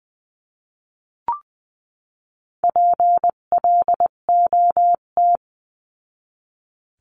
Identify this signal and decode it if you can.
Morse code, heard as a keyed tone, about 20 wpm, 700 Hz. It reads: PLOT
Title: 258 Lesson 42c - Words 16-20wpm